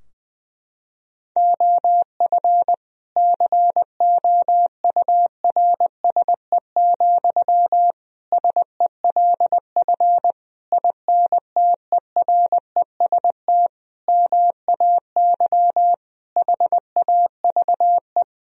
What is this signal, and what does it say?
Morse code, heard as a keyed tone, about 21 wpm, 700 Hz. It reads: OF COURSE, SELF INTEREST MAY HAVE
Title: Sentences from Top 1000 Words - Review Part 2 - 20wpm